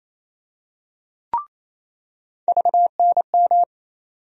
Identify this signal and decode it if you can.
Morse code, heard as a keyed tone, about 28 wpm, 700 Hz. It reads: VNM
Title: USA Call Sign Postfixes - 3 Letters 28wpm